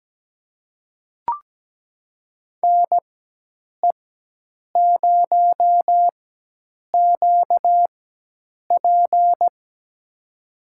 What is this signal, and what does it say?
Morse code, heard as a keyed tone, about 17 wpm, spaced as for 8 wpm, 700 Hz. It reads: NE0QP
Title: USA Call Signs - 5 Characters - 4x Character Spacing - 17wpm